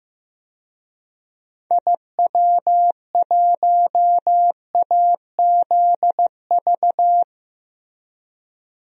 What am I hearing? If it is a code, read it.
Morse code, heard as a keyed tone, about 15 wpm, 700 Hz. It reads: IW1AZV
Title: DXCC-100 Call Signs - 6 Characters - Repeated 3x - 15wpm